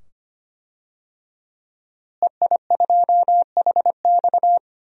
Morse code, heard as a keyed tone, about 25 wpm, 700 Hz. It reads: EI2HX
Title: DXCC-100 Call Signs - 5 Characters - Repeated 2x - 25wpm